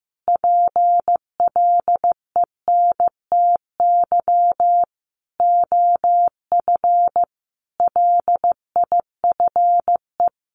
Morse code, heard as a keyed tone, about 15 wpm, 700 Hz. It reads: PLENTY OF LIFE
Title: Sentences from Top 1300 Words 15wpm